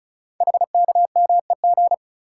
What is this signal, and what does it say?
Morse code, heard as a keyed tone, about 36 wpm, 700 Hz. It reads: HKMEG
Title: Code Groups - 5 Letters - 35wpm